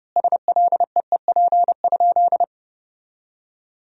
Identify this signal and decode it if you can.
Morse code, heard as a keyed tone, about 32 wpm, 700 Hz. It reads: SLEEP?
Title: 4x Word-Spacing - Sentences from Top 500 Words - 30wpm